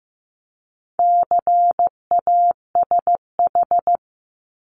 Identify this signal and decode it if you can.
Morse code, heard as a keyed tone, about 15 wpm, 700 Hz. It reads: CASH
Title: Top 501-750 Words 15wpm